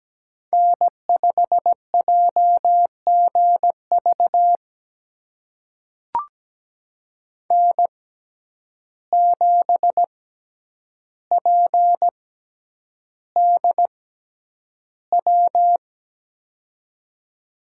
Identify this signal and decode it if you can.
Morse code, heard as a keyed tone, about 17 wpm, spaced as for 9 wpm, 700 Hz. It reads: N5JGV N7PDW
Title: USA Call Signs - 5 Characters - 6x Character Spacing - 17wpm